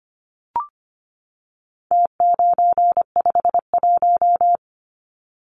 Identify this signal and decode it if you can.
Morse code, heard as a keyed tone, about 25 wpm, 700 Hz. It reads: T951
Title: Numbers - Contest Style - 4-Digit 25wpm